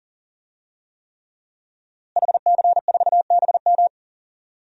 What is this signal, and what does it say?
Morse code, heard as a keyed tone, about 40 wpm, 700 Hz. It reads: HC4BM
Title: DXCC-100 Call Signs - 5 Characters - Repeated 3x - 40wpm